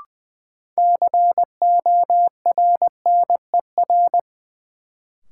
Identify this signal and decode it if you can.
Morse code, heard as a keyed tone, about 20 wpm, 700 Hz. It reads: CORNER
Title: Sentences from Top 1600 Words - New Word Drill - 20wpm